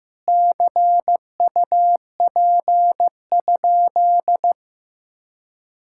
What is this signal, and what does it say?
Morse code, heard as a keyed tone, about 15 wpm, 700 Hz. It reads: CUP?
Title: Sentences from Top 1600 Words 15wpm